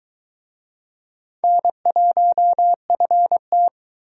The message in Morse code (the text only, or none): N1FT